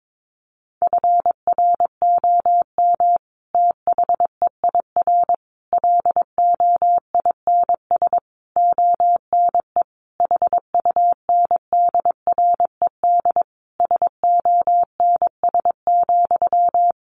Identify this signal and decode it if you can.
Morse code, heard as a keyed tone, about 22 wpm, 700 Hz. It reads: FROM THEIR LOINS ONE HUNDRED SONS,
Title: A Sound Of Thunder 22wpm